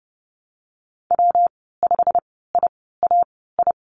Morse code, heard as a keyed tone, about 30 wpm, spaced as for 17 wpm, 700 Hz. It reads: W5IAI
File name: Lesson 19g - Letter w - Call Signs 30wpm